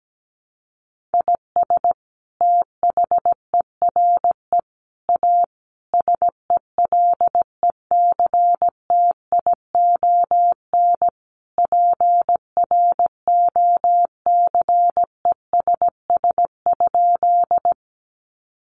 Morse code, heard as a keyed tone, about 17 wpm, 700 Hz. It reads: IS THERE A SELECTION PROCESS?
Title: Sentences from Top 1400 Words 17wpm